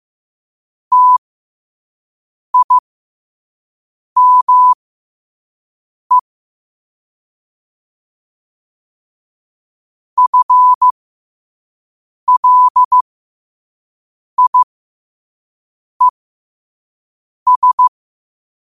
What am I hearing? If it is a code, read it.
Morse code, heard as a keyed tone, about 15 wpm, spaced as for 5 wpm, 1000 Hz. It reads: TIME FLIES
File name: qotdfiles / 5wpm